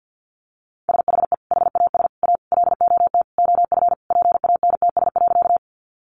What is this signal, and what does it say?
Morse code, heard as a keyed tone, about 50 wpm, 700 Hz. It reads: SHE HAS A LOT OF PANTS.
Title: Additional Easy Phrases 50wpm